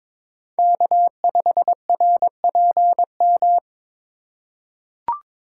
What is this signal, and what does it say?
Morse code, heard as a keyed tone, about 22 wpm, 700 Hz. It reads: K5RPM
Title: USA Call Signs - 5 Characters - 3x Character Spacing - 22wpm